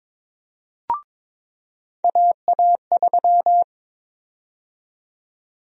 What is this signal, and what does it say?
Morse code, heard as a keyed tone, about 22 wpm, 700 Hz. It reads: AA3
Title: USA Call Sign Prefixes - 22wpm